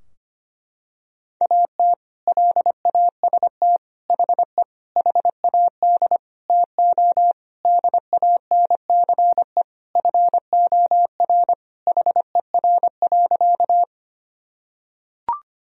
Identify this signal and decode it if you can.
Morse code, heard as a keyed tone, about 25 wpm, 700 Hz. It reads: AT LAST HE HAD TO DANCE FOR HER.
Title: Sentences from Top 1000 Words - Review Part 4 - 25wpm